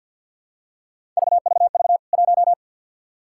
Morse code, heard as a keyed tone, about 52 wpm, 700 Hz. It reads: VVV 1
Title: K1USN SST QSO Practice 50wpm